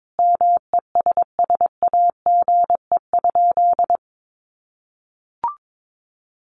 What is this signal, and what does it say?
Morse code, heard as a keyed tone, about 22 wpm, 700 Hz. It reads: MESSAGE?
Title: Sentences from Top 1100 Words 22wpm